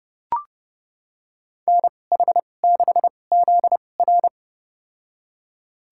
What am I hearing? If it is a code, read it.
Morse code, heard as a keyed tone, about 30 wpm, spaced as for 22 wpm, 700 Hz. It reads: NH6ZR
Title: USA Call Signs - 5 Characters - 2x Character Spacing - 30wpm